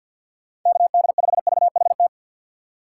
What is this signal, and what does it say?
Morse code, heard as a keyed tone, about 54 wpm, 700 Hz. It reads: KD5VHT
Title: USA Call Signs - 6 Characters - 6x Character Spacing - 50wpm